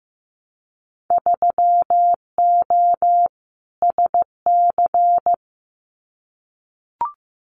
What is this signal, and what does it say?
Morse code, heard as a keyed tone, about 15 wpm, 700 Hz. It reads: 3O SC
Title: Winter Field Day Exchanges Encore 15wpm